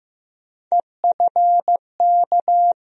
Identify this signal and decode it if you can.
Morse code, heard as a keyed tone, about 16 wpm, 700 Hz. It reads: EFK
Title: USA Call Sign Postfixes - 3 Letters 15wpm